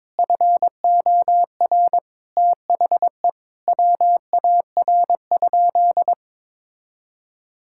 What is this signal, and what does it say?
Morse code, heard as a keyed tone, about 22 wpm, 700 Hz. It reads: FOR THE WAR?
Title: Sentences from Top 1000 Words - Review Part 5 - 22wpm